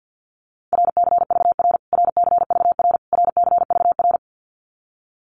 Morse code, heard as a keyed tone, about 50 wpm, 700 Hz. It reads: RCVR RCVR RCVR
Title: Abbreviations Repeated 3x - 50wpm